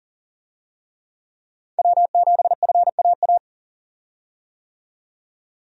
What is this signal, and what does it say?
Morse code, heard as a keyed tone, about 37 wpm, 700 Hz. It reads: W7FAA
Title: USA Call Signs - 5 Characters - 40wpm